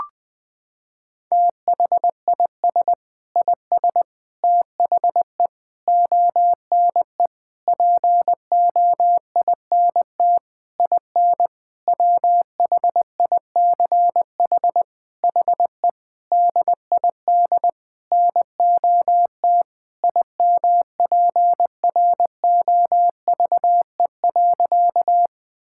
Morse code, heard as a keyed tone, about 20 wpm, 700 Hz. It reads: THIS IS THE ONE POINT IN WHICH HE DID NOT IMPROVE.
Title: Sentences from Top 1000 Words - Review Part 5 - 20wpm